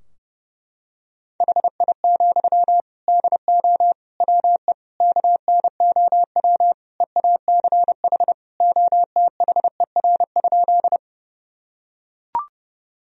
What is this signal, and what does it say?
Morse code, heard as a keyed tone, about 29 wpm, 700 Hz. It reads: HI, DO WE KNOW EACH OTHER?
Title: Sentences from Top 1000 Words - Review Part 3 - 30wpm